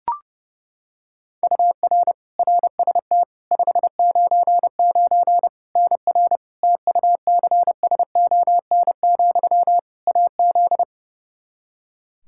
Morse code, heard as a keyed tone, about 32 wpm, 700 Hz. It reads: UR RST 599 NR TUCSON, AZ
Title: Practice QSO - Exchange 1 - Beginning - 30wpm